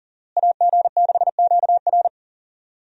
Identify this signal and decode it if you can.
Morse code, heard as a keyed tone, about 40 wpm, 690 Hz. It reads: AG6QR
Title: USA Call Signs - 5 Characters - Repeated 3x - 40wpm